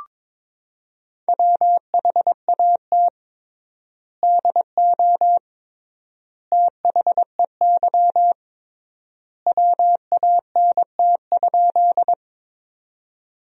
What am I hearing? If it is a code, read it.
Morse code, heard as a keyed tone, about 22 wpm, 700 Hz. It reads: WHAT DO THEY WANT?
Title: 3x Word-Spacing - Sentences from Top 200 Words - 22wpm